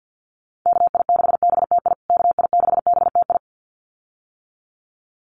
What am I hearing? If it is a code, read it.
Morse code, heard as a keyed tone, about 50 wpm, 700 Hz. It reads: KI6BTI KI6BTI
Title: USA Call Signs Repeated 2x 50wpm